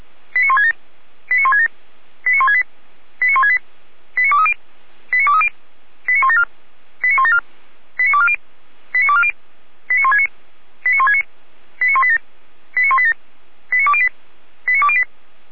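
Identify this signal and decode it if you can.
Five-tone selective call (ZVEI), heard as quick runs of five tones, tones of about 70 ms. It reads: 89167 89167 89168 89168 89230 89230 89174 89174 89240 89240 89169 89169 89187 89187 89208 89208
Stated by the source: ZVEI five-tone selective call 89160 received on SDR, from Feuerwehr Orlamünde fire brigade server.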